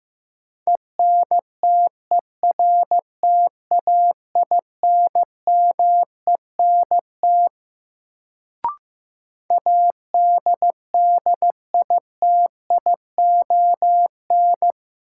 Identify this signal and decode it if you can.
Morse code, heard as a keyed tone, about 15 wpm, 700 Hz. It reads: ENTERTAINMENT ADDITION
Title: Sentences from Top 1000 Words - Review Part 6 - New Word Drill - 15wpm